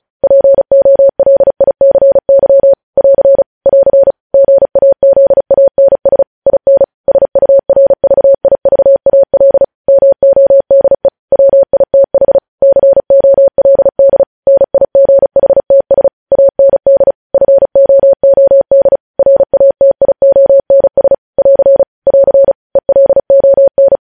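Morse code, heard as a keyed tone, about 36 wpm, 550 Hz. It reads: POLICY <AR> <AR> GAZANS IN SURVIVAL MODE WITH COLD NIGHTS AND FOOD RATIONS <AR> <AR> ELON
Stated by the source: Morse code news headlines at 35.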